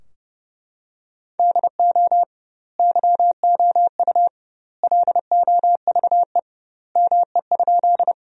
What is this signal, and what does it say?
Morse code, heard as a keyed tone, about 32 wpm, 700 Hz. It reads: DO YOU LOVE ME?
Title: 2x Word-Spacing - Sentences from Top 400 Words - 30wpm